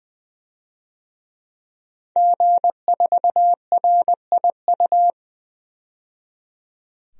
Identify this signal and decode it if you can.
Morse code, heard as a keyed tone, about 20 wpm, 700 Hz. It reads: G4RIU
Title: DXCC-100 Call Signs - 5 Characters - 20wpm